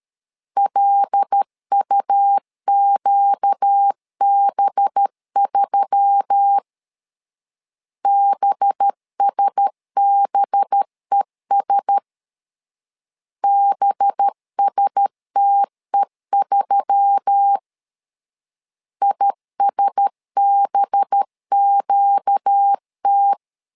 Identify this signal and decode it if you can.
Morse code, heard as a keyed tone, about 13 wpm, 800 Hz. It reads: LUQB3 BSBES BSTE3 ISBQT
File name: LEZIONI / 5 LBVC / 5 LB 12wpm 1906